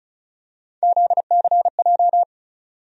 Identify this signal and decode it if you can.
Morse code, heard as a keyed tone, about 36 wpm, 700 Hz. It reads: ZCJ
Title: USA Call Sign Postfixes - 3 Letters 35wpm